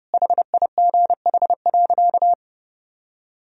Morse code, heard as a keyed tone, about 32 wpm, 700 Hz. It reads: HIGH.
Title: Sentences from Top 1000 Words - Review Part 1 - 30wpm